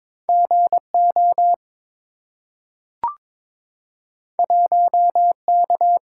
Morse code, heard as a keyed tone, about 22 wpm, 700 Hz. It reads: GO 1K
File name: practice / 22wpm / Two Letter-Number Combinations - 22wpm